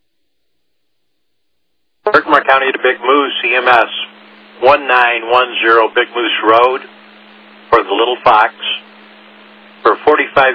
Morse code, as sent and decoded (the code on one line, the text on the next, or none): none